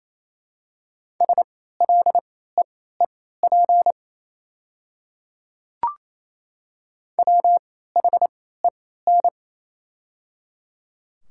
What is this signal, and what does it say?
Morse code, heard as a keyed tone, about 28 wpm, spaced as for 16 wpm, 700 Hz. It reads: SLEEP WHEN
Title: 3x Character Spacing - Top 500 Words - 28wpm